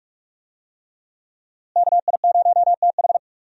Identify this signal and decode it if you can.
Morse code, heard as a keyed tone, about 42 wpm, 700 Hz. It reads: KI0TH
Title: USA Call Signs - 5 Characters - Repeated 3x - 45wpm